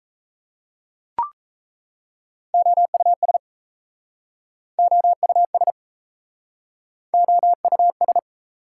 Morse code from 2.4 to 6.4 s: OUS OUS

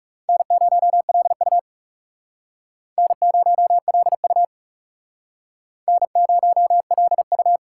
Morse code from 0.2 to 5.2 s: N0LU N0LU

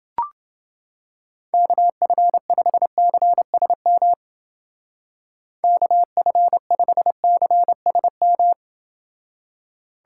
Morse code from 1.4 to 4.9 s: KF5CSM

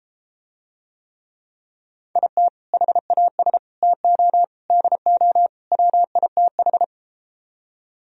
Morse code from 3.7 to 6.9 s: TO DO WITH

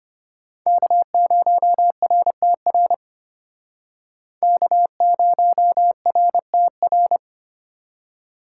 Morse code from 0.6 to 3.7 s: K0RTR

